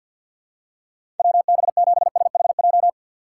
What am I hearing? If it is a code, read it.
Morse code, heard as a keyed tone, about 52 wpm, 700 Hz. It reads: WB7SHJ
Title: USA Call Signs - 6 Characters - 4x Character Spacing - 50wpm